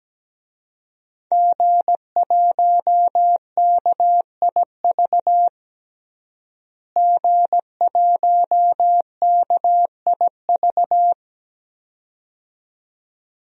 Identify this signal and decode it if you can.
Morse code, heard as a keyed tone, about 17 wpm, 700 Hz. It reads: G1KIV G1KIV
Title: DXCC-100 Call Signs - 5 Characters - Repeated 2x - 17wpm